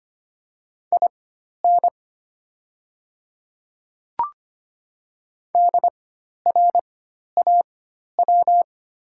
Morse code, heard as a keyed tone, about 25 wpm, spaced as for 12 wpm, 700 Hz. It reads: IN DRAW
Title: 4x Character Spacing - Top 300 Words - 25wpm